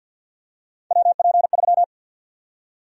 Morse code, heard as a keyed tone, about 51 wpm, 700 Hz. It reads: WP3
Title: USA Call Sign Prefixes - Repeated 2x - 50wpm